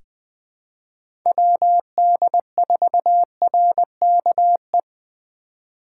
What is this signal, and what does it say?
Morse code, heard as a keyed tone, about 20 wpm, 700 Hz. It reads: WD4RKE
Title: USA Call Signs - 6 Characters - 2x Character Spacing - 20wpm